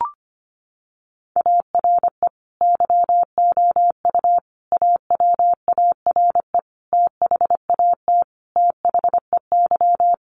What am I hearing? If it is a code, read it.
Morse code, heard as a keyed tone, about 25 wpm, 700 Hz. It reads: ARE YOU AWARE THAT THEY
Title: Sentences from Top 1200 Words 25wpm